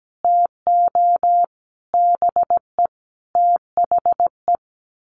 Morse code, heard as a keyed tone, about 17 wpm, 700 Hz. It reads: TO BE THE
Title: Sets of 5 Words Encore2 17wpm